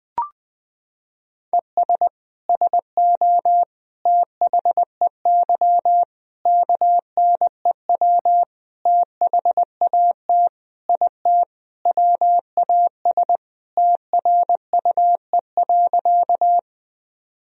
Morse code, heard as a keyed tone, about 21 wpm, 700 Hz. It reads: ES SO THEY KNEW THAT IT WAS TRUE.
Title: Sentences from Top 400 Words and Abbreviations 20wpm